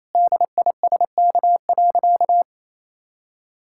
Morse code, heard as a keyed tone, about 28 wpm, 700 Hz. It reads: DISK.
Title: Sentences from Top 1000 Words - Review Part 6 - 28wpm